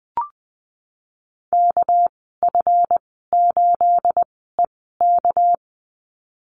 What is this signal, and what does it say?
Morse code, heard as a keyed tone, about 20 wpm, spaced as for 14 wpm, 700 Hz. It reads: KF8EK
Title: 219 Lesson 35g - Number 0 - Call Signs 20wpm